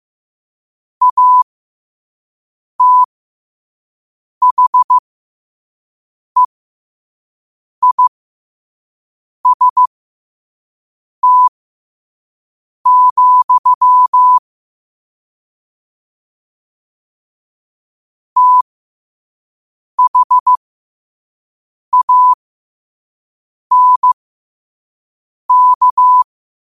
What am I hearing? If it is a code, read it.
Morse code, heard as a keyed tone, about 15 wpm, spaced as for 5 wpm, 1000 Hz. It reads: ATHEIST, THANK
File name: qotdfiles / 5wpm